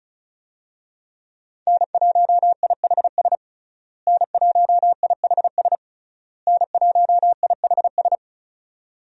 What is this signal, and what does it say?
Morse code, heard as a keyed tone, about 34 wpm, 700 Hz. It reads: N1IHS N1IHS N1IHS
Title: USA Call Signs - 5 Characters - Repeated 3x - 35wpm